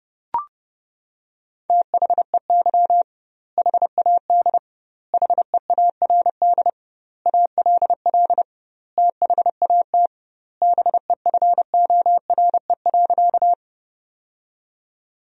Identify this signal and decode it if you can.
Morse code, heard as a keyed tone, about 32 wpm, 700 Hz. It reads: THEY HAD HEARD ALL THAT BEFORE.
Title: 2x Word-Spacing - Sentences from Top 400 Words - 30wpm